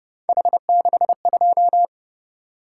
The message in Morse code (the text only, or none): H62